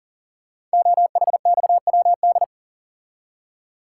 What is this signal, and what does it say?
Morse code, heard as a keyed tone, about 41 wpm, 690 Hz. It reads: OHXWD
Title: Code Groups - 5 Letters - 40wpm